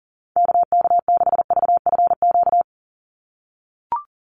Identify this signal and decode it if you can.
Morse code, heard as a keyed tone, about 40 wpm, 700 Hz. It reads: KK6VFQ
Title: USA Call Signs 40wpm